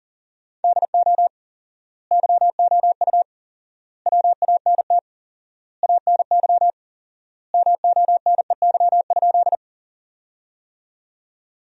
Morse code, heard as a keyed tone, about 40 wpm, 700 Hz. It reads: DO YOU WANT ANY MONEY?
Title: 4x Word-Spacing - Sentences from Top 400 Words - 40wpm